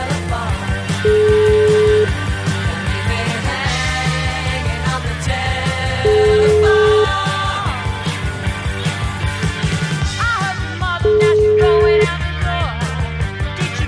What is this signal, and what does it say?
Ringback tone (Europe, 425 Hz), with noise.